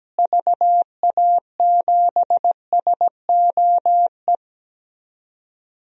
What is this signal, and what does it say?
Morse code, heard as a keyed tone, about 17 wpm, 700 Hz. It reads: VA7SOE
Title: DXCC-100 Call Signs - 6 Characters - Repeated 2x - 17wpm